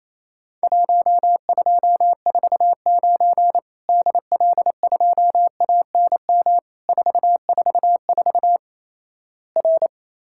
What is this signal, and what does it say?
Morse code, heard as a keyed tone, about 27 wpm, 700 Hz. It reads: 1249 DL2ANM 444 R
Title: Worked All Europe DX Contest - QTC Exchange - 28wpm